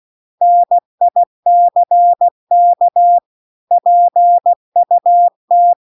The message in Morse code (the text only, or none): NICK PUT